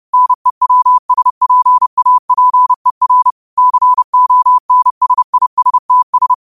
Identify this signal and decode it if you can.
Morse code, heard as a keyed tone, about 30 wpm, 1000 Hz. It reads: NEWSPAPER CONSISTS
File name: qotdfiles / 30wpm